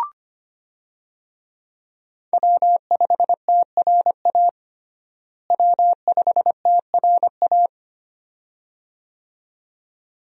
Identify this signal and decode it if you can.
Morse code, heard as a keyed tone, about 25 wpm, 700 Hz. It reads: W5TRA W5TRA
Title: USA Call Signs - 5 Characters - Repeated 2x - 25wpm